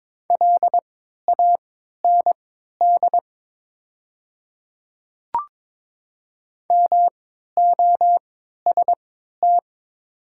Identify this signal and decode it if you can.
Morse code, heard as a keyed tone, about 22 wpm, spaced as for 13 wpm, 690 Hz. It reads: LAND MOST
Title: 3x Character Spacing - Top 200 Words - 22wpm